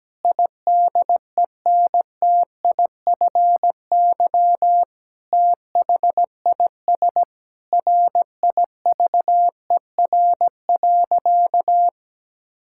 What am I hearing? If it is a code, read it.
Morse code, heard as a keyed tone, about 17 wpm, 700 Hz. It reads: IDENTIFY THIS RIVER.
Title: Sentences from Top 1000 Words - Review Part 6 - 17wpm